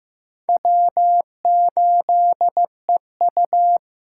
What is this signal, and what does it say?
Morse code, heard as a keyed tone, about 18 wpm, 700 Hz. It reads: W8EU
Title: Speed Racing - USA Call Signs 15wpm